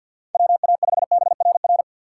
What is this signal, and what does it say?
Morse code, heard as a keyed tone, about 53 wpm, 700 Hz. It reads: WA5BRR
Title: USA Call Signs - 6 Characters - 3x Character Spacing - 50wpm